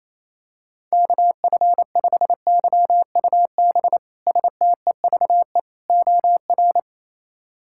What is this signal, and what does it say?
Morse code, heard as a keyed tone, about 28 wpm, 700 Hz. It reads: KF5YUB STEVE OR
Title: K1USN SST QSO Practice - 28wpm